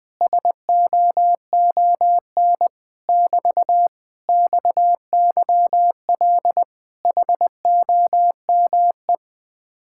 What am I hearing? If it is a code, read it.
Morse code, heard as a keyed tone, about 19 wpm, 700 Hz. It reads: SOON <BT> XYL HOME
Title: Speed-Racing Common QSO Phrases - 20wpm